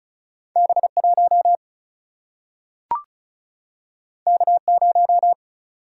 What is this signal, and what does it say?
Morse code, heard as a keyed tone, about 36 wpm, 700 Hz. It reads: B 1 K 0
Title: Single-Letter Single-Number 35wpm